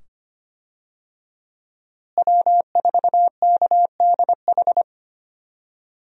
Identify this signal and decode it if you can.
Morse code, heard as a keyed tone, about 25 wpm, 700 Hz. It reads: W4KDH
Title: USA Call Signs - 5 Characters - Repeated 2x - 25wpm